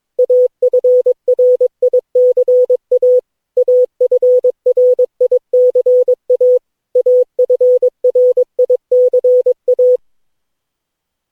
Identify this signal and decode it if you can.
Morse code, heard as a keyed tone, about 22 wpm, 500 Hz. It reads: AFRICA AFRICA AFRICA
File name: cw / Session9 cw training